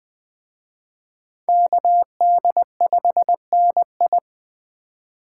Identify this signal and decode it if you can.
Morse code, heard as a keyed tone, about 20 wpm, 700 Hz. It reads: KD5NI
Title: Speed Racing - USA 2x2 Call Signs 17wpm